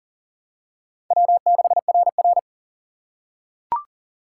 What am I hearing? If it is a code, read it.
Morse code, heard as a keyed tone, about 40 wpm, 700 Hz. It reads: W6RR
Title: USA Call Signs - 4 Characters - 3x Character Spacing - 40wpm